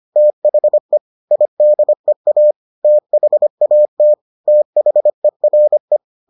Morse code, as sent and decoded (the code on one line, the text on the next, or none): - .... / .. -.. .- / - .... .- - / - .... .-. .
THE IDEA THAT THERE